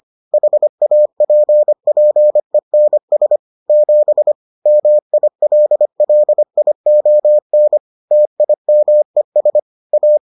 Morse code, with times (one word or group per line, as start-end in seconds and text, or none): 0.3-3.4 HAPPENS
3.7-4.3 7
4.7-7.8 MILLION
8.1-9.6 TIMES
9.9-10.2 A